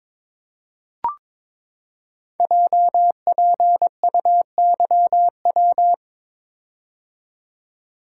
Morse code, heard as a keyed tone, about 23 wpm, 700 Hz. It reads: JPUYW
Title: Code Groups - 5 Letters - 22wpm